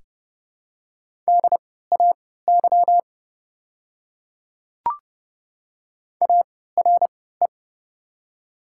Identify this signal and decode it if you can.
Morse code, heard as a keyed tone, about 31 wpm, spaced as for 17 wpm, 700 Hz. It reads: DAY ARE
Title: 3x Character Spacing - Top 100 Words - 30wpm